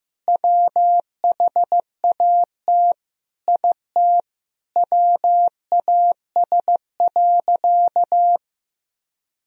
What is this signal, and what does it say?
Morse code, heard as a keyed tone, about 15 wpm, 700 Hz. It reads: WHAT IT WAS.
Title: Sentences from Top 1000 Words - Review Part 3 - 15wpm